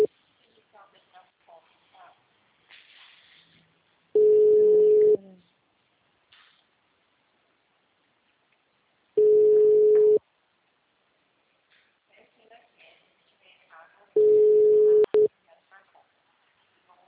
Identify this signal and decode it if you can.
Ringback tone (Europe, 425 Hz), with speech.